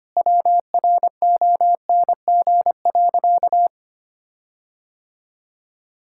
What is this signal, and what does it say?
Morse code, heard as a keyed tone, about 25 wpm, 700 Hz. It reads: WRONG.
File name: practice / 25wpm / Sentences with 9-Word Phrases - 25wpm